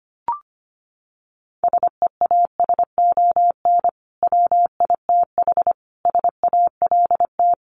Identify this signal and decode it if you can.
Morse code, heard as a keyed tone, about 25 wpm, 700 Hz. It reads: SEASON WITH SALT
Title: Sets of 5 Words 25wpm